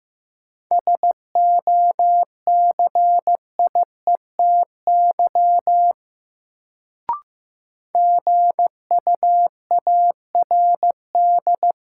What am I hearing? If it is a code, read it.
Morse code, heard as a keyed tone, about 15 wpm, 700 Hz. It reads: SOCIETY GUARD
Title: Sentences from Top 700 Words - New Word Drill - 15wpm